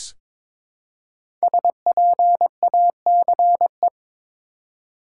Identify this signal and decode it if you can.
Morse code, heard as a keyed tone, about 22 wpm, 700 Hz. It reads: SPACE
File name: 5 Letter Words - First 3 Letters Given - 22wpm